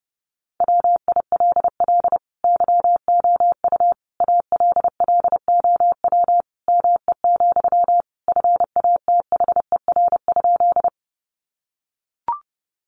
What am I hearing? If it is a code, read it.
Morse code, heard as a keyed tone, about 30 wpm, 700 Hz. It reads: WILL YOU ALLOW ME, FATHER?
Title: Sentences from Top 700 Words 30wpm